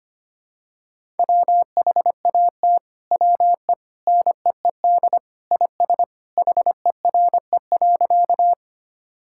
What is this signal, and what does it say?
Morse code, heard as a keyed tone, about 25 wpm, 700 Hz. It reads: WHAT WE NEED IS HERE.
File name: practice / 25wpm / Sentences with 2-Word Phrases - 25wpm